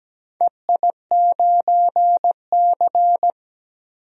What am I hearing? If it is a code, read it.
Morse code, heard as a keyed tone, about 17 wpm, 700 Hz. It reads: EI9C
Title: DXCC-100 Call Signs - 4 Characters - 17wpm